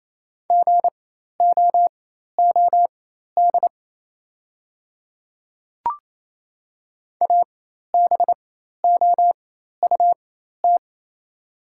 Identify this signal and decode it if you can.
Morse code, heard as a keyed tone, about 27 wpm, spaced as for 13 wpm, 690 Hz. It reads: GOOD ABOUT
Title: 4x Character Spacing - Top 200 Words - 28wpm